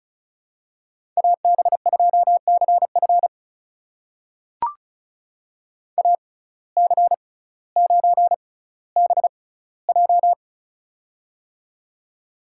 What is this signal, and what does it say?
Morse code, heard as a keyed tone, about 36 wpm, spaced as for 18 wpm, 700 Hz. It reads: AB2CF AC9BJ
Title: USA Call Signs - 5 Characters - 6x Character Spacing - 35wpm